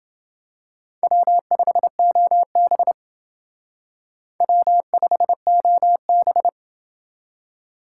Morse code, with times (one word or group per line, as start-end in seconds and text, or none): 1.0-2.9 W5OB
4.4-6.5 W5OB